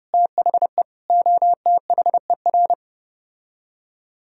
Morse code, heard as a keyed tone, about 33 wpm, 700 Hz. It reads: THE OTHER